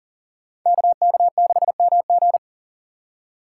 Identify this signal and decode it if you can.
Morse code, heard as a keyed tone, about 38 wpm, 700 Hz. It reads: KK6MG
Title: USA Call Signs - 5 Characters - 2x Character Spacing - 40wpm